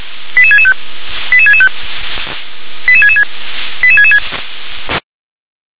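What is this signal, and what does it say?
Five-tone selective call (ZVEI), heard as quick runs of five tones, tones of about 70 ms. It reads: 88665 88665 88666 88666
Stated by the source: ZVEI five-tone selective call received on an SDR.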